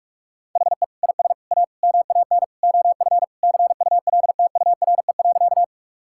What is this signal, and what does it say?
Morse code, heard as a keyed tone, about 46 wpm, 700 Hz. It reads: HE IS A MAN OF CULTURE.
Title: Sentences from Top 1000 Words - Review Part 6 - 45wpm